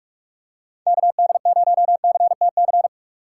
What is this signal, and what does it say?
Morse code, heard as a keyed tone, about 44 wpm, 700 Hz. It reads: KD0CTC